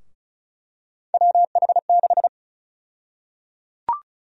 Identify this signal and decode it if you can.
Morse code, heard as a keyed tone, about 34 wpm, 700 Hz. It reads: WH6
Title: USA Call Sign Prefixes - Repeated 2x - 35wpm